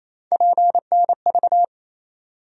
Morse code, heard as a keyed tone, about 28 wpm, 700 Hz. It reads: PNV
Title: USA Call Sign Postfixes - 3 Letters 28wpm